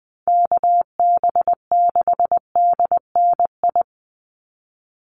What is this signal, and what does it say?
Morse code, heard as a keyed tone, about 20 wpm, 700 Hz. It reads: KB6DNI
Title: USA Call Signs 20wpm